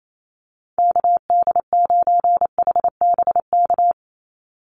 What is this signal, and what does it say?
Morse code, heard as a keyed tone, about 28 wpm, 700 Hz. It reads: KD9HBK
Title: USA Call Signs 28wpm